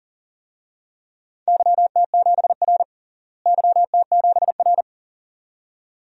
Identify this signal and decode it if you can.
Morse code, heard as a keyed tone, about 40 wpm, 700 Hz. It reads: YT7R YT7R
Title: DXCC-100 Call Signs - 4 Characters - Repeated 2x - 40wpm